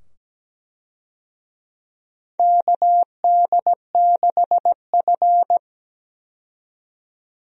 Morse code, heard as a keyed tone, about 17 wpm, 700 Hz. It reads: KD6F
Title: USA Call Signs - 4 Characters - 17wpm